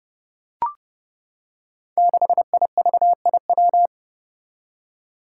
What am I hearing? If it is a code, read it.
Morse code, heard as a keyed tone, about 28 wpm, 700 Hz. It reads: 6IVIW
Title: Five Challenging Letter-Number Combinations 30wpm